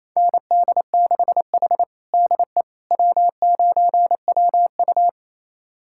Morse code, heard as a keyed tone, about 28 wpm, 700 Hz. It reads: ND6H DE W9WU
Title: USA Callsign DE Callsign - 4 Characters - 28wpm